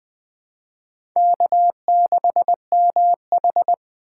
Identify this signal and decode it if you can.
Morse code, heard as a keyed tone, about 20 wpm, 700 Hz. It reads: K6MH
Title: USA Call Signs - 4 Characters - 3x Character Spacing - 20wpm